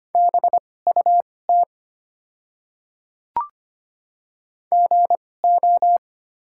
Morse code, heard as a keyed tone, about 25 wpm, spaced as for 18 wpm, 700 Hz. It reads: BUT GO
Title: 2x Character Spacing - Top 200 Words - 25wpm